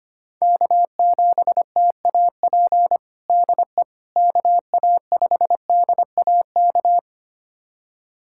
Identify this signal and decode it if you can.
Morse code, heard as a keyed tone, about 25 wpm, 700 Hz. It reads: K7TAP DE KA5DAK
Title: USA Callsign DE Callsign 25wpm